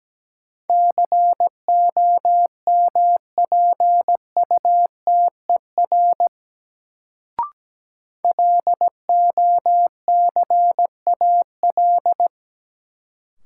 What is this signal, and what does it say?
Morse code, heard as a keyed tone, about 17 wpm, 700 Hz. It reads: COMPUTER LOCAL
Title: Sentences from Top 1000 Words - Review Part 3 - New Word Drill - 17wpm